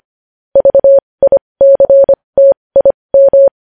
Morse code, heard as a keyed tone, about 25 wpm, spaced as for 20 wpm, 550 Hz. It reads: VICTIM
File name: podcast / News headlines-1x-25x20